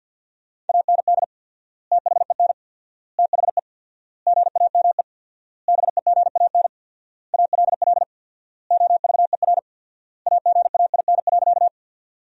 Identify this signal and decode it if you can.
Morse code, heard as a keyed tone, about 47 wpm, 700 Hz. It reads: AND THEN THE GAME BEGAN ALL OVER AGAIN.
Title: 4x Word-Spacing - Sentences from Top 500 Words - 50wpm